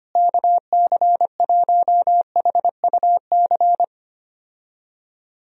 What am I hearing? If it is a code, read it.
Morse code, heard as a keyed tone, about 25 wpm, 700 Hz. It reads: KC1HUC
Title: USA Call Signs - 6 Characters - Repeated 2x - 25wpm